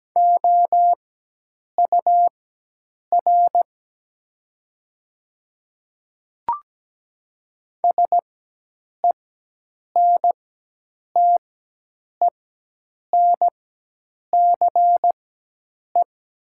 Morse code, heard as a keyed tone, about 17 wpm, spaced as for 8 wpm, 700 Hz. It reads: OUR SENTENCE
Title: 4x Character Spacing - Top 200 Words - 17wpm